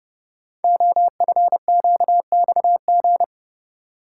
Morse code, heard as a keyed tone, about 30 wpm, 700 Hz. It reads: OFQXG
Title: Code Groups - 5 Letters - 30wpm